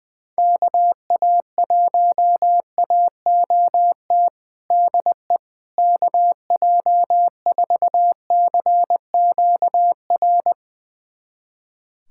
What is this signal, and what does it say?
Morse code, heard as a keyed tone, about 21 wpm, 700 Hz. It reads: KA1AOT DE KJ4CQR
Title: USA Callsign DE Callsign 20wpm